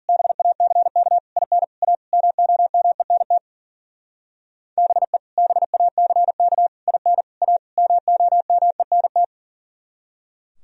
Morse code, heard as a keyed tone, about 51 wpm, 700 Hz. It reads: BACK IN A MOMENT BE BACK IN A MOMENT